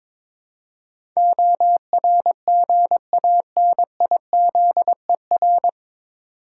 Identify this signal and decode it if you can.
Morse code, heard as a keyed tone, about 22 wpm, 700 Hz. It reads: ORGANIZER
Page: Morse Code Ninja practice 22wpm